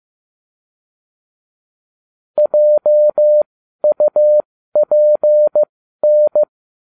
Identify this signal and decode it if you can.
Morse code, heard as a keyed tone, about 15 wpm, spaced as for 12 wpm, 600 Hz. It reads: JUPN